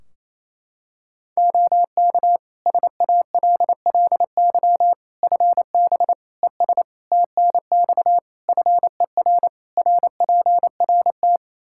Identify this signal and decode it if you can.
Morse code, heard as a keyed tone, about 28 wpm, 700 Hz. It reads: OK SALLY FB ES TNX FER RPRT
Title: Practice QSO - Exchange 2 - Info - 28wpm